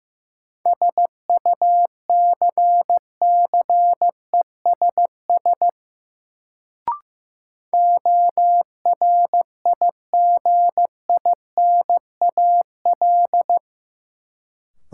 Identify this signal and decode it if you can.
Morse code, heard as a keyed tone, about 15 wpm, 700 Hz. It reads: SUCCESS ORIGINAL
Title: Sentences from Top 900 Words - New Word Drill - 15wpm